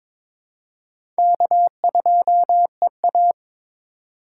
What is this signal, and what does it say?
Morse code, heard as a keyed tone, about 22 wpm, 700 Hz. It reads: K2EA
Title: USA Call Signs - 2x Character Spacing - 22wpm